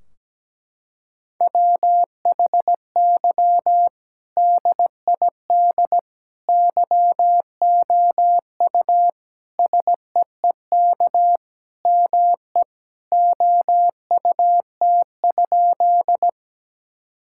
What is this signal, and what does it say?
Morse code, heard as a keyed tone, about 17 wpm, 700 Hz. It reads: WHY DID YOU SEEK ME OUT?
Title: Sentences from Top 1600 Words 17wpm